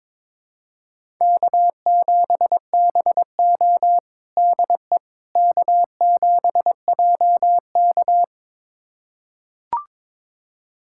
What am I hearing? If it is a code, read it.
Morse code, heard as a keyed tone, about 22 wpm, 700 Hz. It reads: K7BO DE K7JK
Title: USA Callsign DE Callsign - 4 Characters - 22wpm